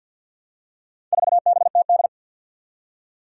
Morse code, heard as a keyed tone, about 52 wpm, 700 Hz. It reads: 4BTD
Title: Four Challenging Letter-Number Combinations 50wpm